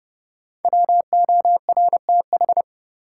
Morse code, heard as a keyed tone, about 30 wpm, 700 Hz. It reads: WORTH
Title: Sentences from Top 800 Words - New Word Drill - 30wpm